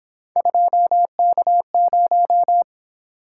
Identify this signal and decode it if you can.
Morse code, heard as a keyed tone, about 26 wpm, 700 Hz. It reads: 2K0
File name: Speed-Racing - Three Letter-Number Combinations - 22wpm